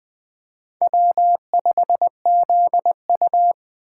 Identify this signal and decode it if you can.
Morse code, heard as a keyed tone, about 21 wpm, 700 Hz. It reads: W5ZU
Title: Speed Racing - USA Call Signs 17wpm